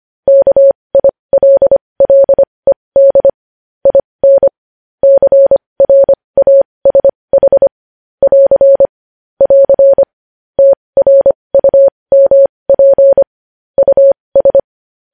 Morse code, heard as a keyed tone, about 25 wpm, spaced as for 20 wpm, 550 Hz. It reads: KILLED IN CRASH <AR> <AR> TRUMP US
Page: news-headlines-in-morse-code-at-20-wpm